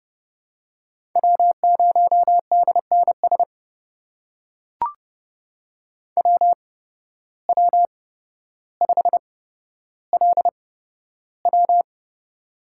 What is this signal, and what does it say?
Morse code, heard as a keyed tone, about 31 wpm, spaced as for 13 wpm, 700 Hz. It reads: W0DNS WW5LW
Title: USA Call Signs - 5 Characters - 8x Character Spacing - 30wpm